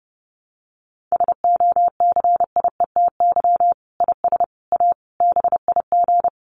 Morse code, heard as a keyed tone, about 28 wpm, 700 Hz. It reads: SOCIETY IS A BIG
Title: Sentences from Top 1500 Words 30wpm